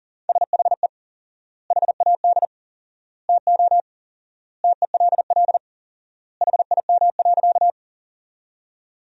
Morse code, heard as a keyed tone, about 40 wpm, 700 Hz. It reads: SHE HAD TO TELL HIM.
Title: 4x Word-Spacing - Sentences from Top 200 Words - 40wpm